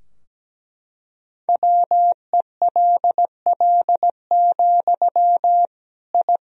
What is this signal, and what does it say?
Morse code, heard as a keyed tone, about 17 wpm, 700 Hz. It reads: WELL, I
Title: Sentences from Top 1000 Words - Review Part 2 - 17wpm